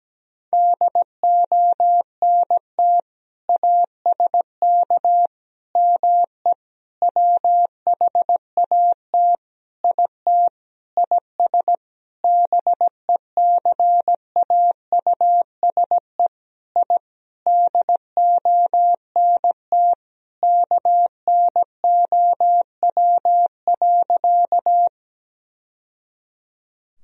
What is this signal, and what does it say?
Morse code, heard as a keyed tone, about 17 wpm, 700 Hz. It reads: DONT ASK ME WHAT IT IS BECAUSE I DONT KNOW.
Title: Sentences from Top 1000 Words - Review Part 3 - 17wpm